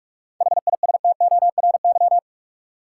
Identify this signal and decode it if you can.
Morse code, heard as a keyed tone, about 45 wpm, 700 Hz. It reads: HISTORY
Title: Seven Letter Words - 45wpm